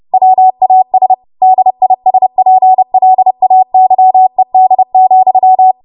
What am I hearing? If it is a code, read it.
Morse code, heard as a keyed tone, about 30 wpm, 750 Hz. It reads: WAS DISPLAYED,
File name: Morse / 30WPM